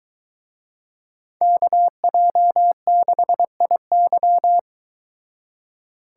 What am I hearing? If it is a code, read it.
Morse code, heard as a keyed tone, about 23 wpm, 700 Hz. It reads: KJ6IY